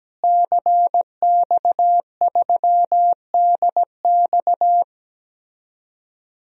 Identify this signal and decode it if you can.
Morse code, heard as a keyed tone, about 17 wpm, 700 Hz. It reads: CX3DX
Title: DXCC-100 Call Signs - 5 Characters - Repeated 3x - 17wpm